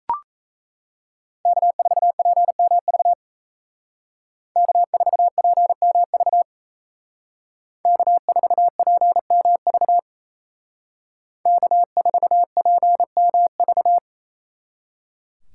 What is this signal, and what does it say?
Morse code, heard as a keyed tone, about 36 wpm, 700 Hz. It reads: K4PMV K4PMV K4PMV K4PMV